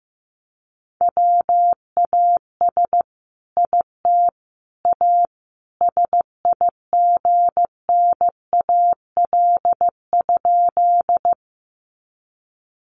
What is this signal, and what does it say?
Morse code, heard as a keyed tone, about 15 wpm, 700 Hz. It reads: WAS IT A SIGNAL?
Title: Sentences from Top 1400 Words 15wpm